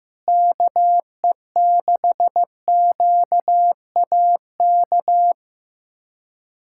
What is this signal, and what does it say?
Morse code, heard as a keyed tone, about 15 wpm, 700 Hz. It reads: KE6QAK
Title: USA Callsign DE Callsign 15wpm